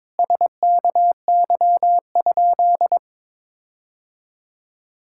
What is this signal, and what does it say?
Morse code, heard as a keyed tone, about 21 wpm, 700 Hz. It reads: SKY?
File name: practice / 22wpm / Sentences with 4-Word Phrases - 22wpm